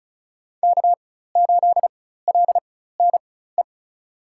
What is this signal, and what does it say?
Morse code, heard as a keyed tone, about 34 wpm, spaced as for 16 wpm, 700 Hz. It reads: K8LNE
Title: USA Call Signs - 5 Characters - 4x Character Spacing - 35wpm